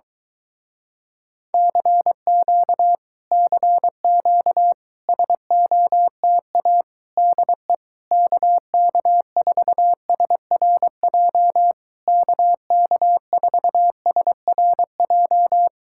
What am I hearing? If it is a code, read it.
Morse code, heard as a keyed tone, about 23 wpm, 700 Hz. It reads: CQ CQ SOTA DE KK4SRJ KK4SRJ